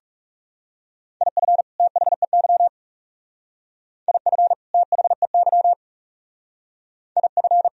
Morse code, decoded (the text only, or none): IF THEY IF THEY IF